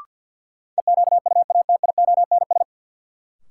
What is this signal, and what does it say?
Morse code, heard as a keyed tone, about 52 wpm, 700 Hz. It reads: EQUATIONS
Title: Sets of 5 Words - New Word Drill - 50wpm